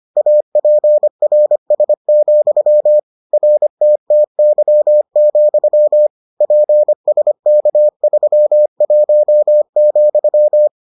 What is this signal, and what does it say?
Morse code, heard as a keyed tone, about 25 wpm, 600 Hz. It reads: APRS, RTTY, PSK31,